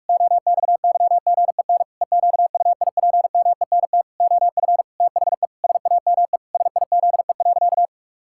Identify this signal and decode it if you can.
Morse code, heard as a keyed tone, about 41 wpm, 700 Hz. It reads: OXYGEN EQUIPMENT OF THE SAME SIZE.
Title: Pangram Challenge v2 - 45wpm